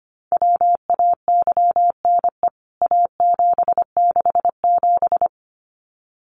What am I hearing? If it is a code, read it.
Morse code, heard as a keyed tone, about 25 wpm, 700 Hz. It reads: WAYNE A767
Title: CWT Exchange 25wpm